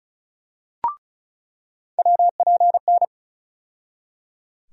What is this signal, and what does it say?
Morse code, heard as a keyed tone, about 36 wpm, 700 Hz. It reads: WPN
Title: USA Call Sign Postfixes - 3 Letters 35wpm